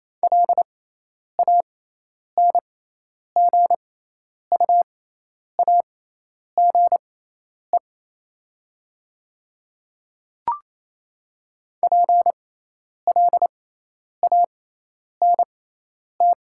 Morse code, heard as a keyed tone, about 28 wpm, spaced as for 10 wpm, 700 Hz. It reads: LANGUAGE PLANT
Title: 6x Character Spacing - Top 500 Words - 28wpm